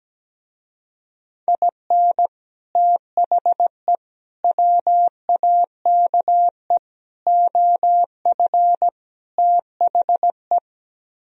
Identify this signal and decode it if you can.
Morse code, heard as a keyed tone, about 17 wpm, 700 Hz. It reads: IN THE WAKE OF THE
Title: Speed-Racing - Sets of 5 Words 17wpm